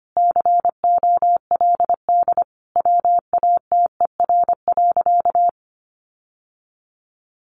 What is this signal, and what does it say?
Morse code, heard as a keyed tone, about 25 wpm, 700 Hz. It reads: COLD WATER.
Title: Sentences from Top 1200 Words 25wpm